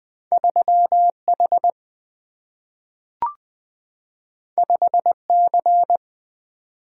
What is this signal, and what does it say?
Morse code, heard as a keyed tone, about 20 wpm, 700 Hz. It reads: 3 H 5 C
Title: Single-Letter Single-Number 20wpm